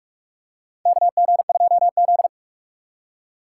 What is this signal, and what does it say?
Morse code, heard as a keyed tone, about 42 wpm, 700 Hz. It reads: KG2Z
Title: USA Call Signs - 2x Character Spacing - 45wpm